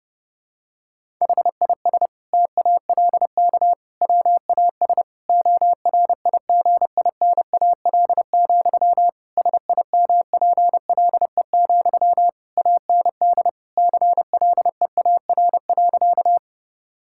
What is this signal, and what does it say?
Morse code, heard as a keyed tone, about 30 wpm, 700 Hz. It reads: HIS TALK WAS ORIGINAL, SIMPLE, AND CLEAR.
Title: Sentences from Top 1000 Words - Review Part 6 - 30wpm